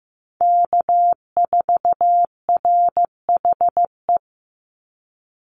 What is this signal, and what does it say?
Morse code, heard as a keyed tone, about 15 wpm, 700 Hz. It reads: K4RHE
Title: USA Call Signs 15wpm